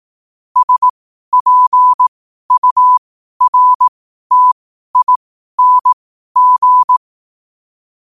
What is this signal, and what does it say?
Morse code, heard as a keyed tone, about 18 wpm, spaced as for 12 wpm, 1000 Hz. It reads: SPURTING
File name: qotdfiles / 12wpm